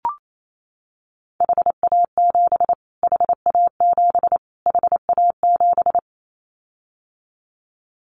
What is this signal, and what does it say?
Morse code, heard as a keyed tone, about 28 wpm, 700 Hz. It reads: HA7 HA7 HA7
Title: Callsign Prefixes for Top 25 DXCC - Repeated 3x - 28wpm